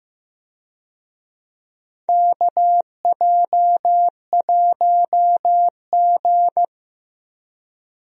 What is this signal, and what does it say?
Morse code, heard as a keyed tone, about 15 wpm, 700 Hz. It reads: KJ1G
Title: USA Call Signs - 4 Characters - Repeated 3x - 15wpm